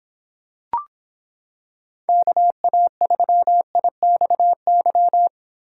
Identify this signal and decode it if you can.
Morse code, heard as a keyed tone, about 26 wpm, 700 Hz. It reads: KA3IXY